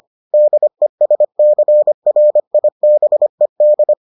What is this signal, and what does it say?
Morse code, heard as a keyed tone, about 25 wpm, 600 Hz. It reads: DESCRIBED